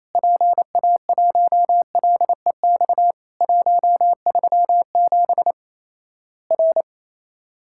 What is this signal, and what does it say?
Morse code, heard as a keyed tone, about 28 wpm, 700 Hz. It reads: PA1LEX 137 R
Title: Worked All Europe DX Contest - QTC Exchange - 28wpm